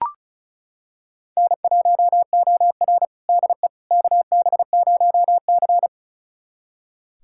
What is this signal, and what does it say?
Morse code, heard as a keyed tone, about 36 wpm, 700 Hz. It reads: N1OR DE KB0C
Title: USA Callsign DE Callsign - 4 Characters - 35wpm